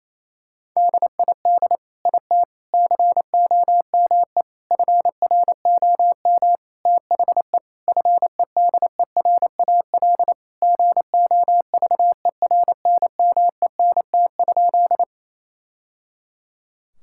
Morse code, heard as a keyed tone, about 28 wpm, 700 Hz. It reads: DID IT COME FROM THE FEDERAL GOVERNMENT?
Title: Sentences from Top 1000 Words - Review Part 5 - 28wpm